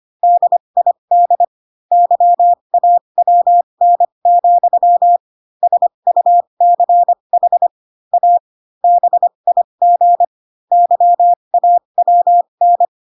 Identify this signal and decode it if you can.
Morse code, heard as a keyed tone, about 25 wpm, spaced as for 22 wpm, 700 Hz. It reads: DID YAWN, SUCH A BIG YAWN